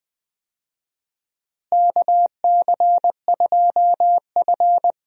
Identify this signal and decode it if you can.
Morse code, heard as a keyed tone, about 20 wpm, 700 Hz. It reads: KC2F
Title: USA Call Signs - 4 Characters - Repeated 2x - 20wpm